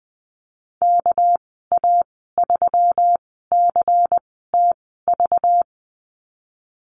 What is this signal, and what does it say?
Morse code, heard as a keyed tone, about 19 wpm, 700 Hz. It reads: K A 3 C T V
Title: 231 Lesson 37g - Letter x - Call Signs 20wpm